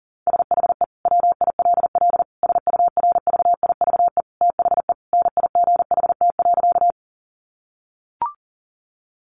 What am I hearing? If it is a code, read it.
Morse code, heard as a keyed tone, about 40 wpm, 700 Hz. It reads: SHE WILL SURVIVE THE NIGHT.
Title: Sentences from Top 1300 Words 40wpm